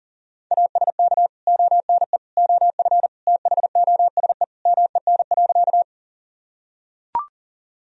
Morse code, heard as a keyed tone, about 40 wpm, 700 Hz. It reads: ASK ONE OF THOSE MEN.
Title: Sentences from Top 1000 Words - Review Part 1 - 40wpm